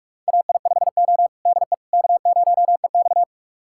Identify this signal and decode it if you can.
Morse code, heard as a keyed tone, about 41 wpm, 700 Hz. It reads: AI5O DE K0EX
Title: USA Callsign DE Callsign - 4 Characters - 45wpm